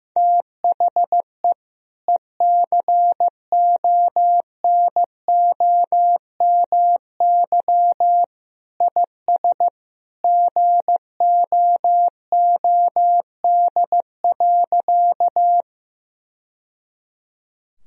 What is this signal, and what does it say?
Morse code, heard as a keyed tone, about 15 wpm, 700 Hz. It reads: THE ECONOMY IS GOOD.
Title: Sentences from Top 1000 Words - Review Part 3 - 15wpm